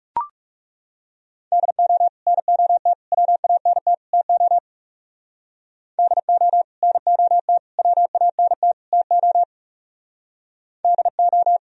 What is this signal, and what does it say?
Morse code, heard as a keyed tone, about 37 wpm, 700 Hz. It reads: DO NOT WANT TO DO NOT WANT TO DO
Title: Speed-Racing - Sets of 4 Words 30wpm